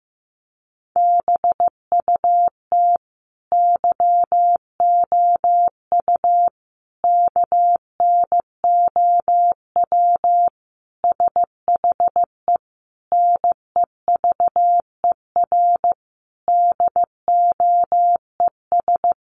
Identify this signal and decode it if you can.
Morse code, heard as a keyed tone, about 14 wpm, 700 Hz. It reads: BUT YOU KNOW SHE NEVER DOES
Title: Sentences from Top 1400 Words 15wpm